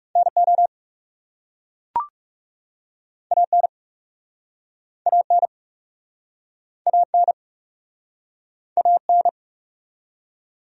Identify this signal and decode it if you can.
Morse code, heard as a keyed tone, about 42 wpm, 700 Hz. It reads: NO AN AN AN AN